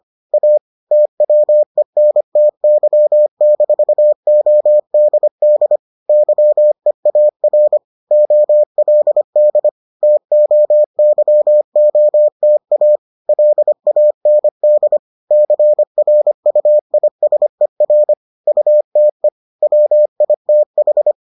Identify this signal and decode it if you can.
Morse code, heard as a keyed tone, about 25 wpm, 600 Hz. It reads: A TWENTY-ODD YEAR OLD TOYOTA LAND CRUISER UTE WITH